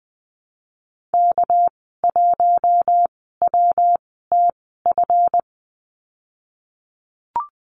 Morse code, heard as a keyed tone, about 20 wpm, spaced as for 14 wpm, 700 Hz. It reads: K1WTF
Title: 219 Lesson 35g - Number 0 - Call Signs 20wpm